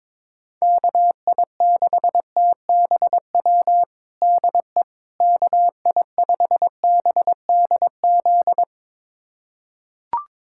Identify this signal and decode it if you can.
Morse code, heard as a keyed tone, about 22 wpm, 700 Hz. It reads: KI6TBW DE KI5BDZ
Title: USA Callsign DE Callsign - 6 Characters - 22wpm